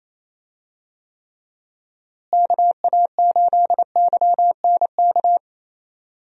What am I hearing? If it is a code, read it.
Morse code, heard as a keyed tone, about 27 wpm, 700 Hz. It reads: KA8YNK
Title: USA Call Signs - 6 Characters - Repeated 3x - 28wpm